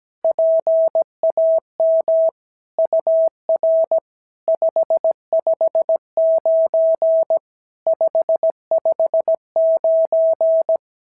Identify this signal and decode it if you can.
Morse code, heard as a keyed tone, about 17 wpm, 650 Hz. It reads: PAM UR 559 559
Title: SOTA QSO Practice 17wpm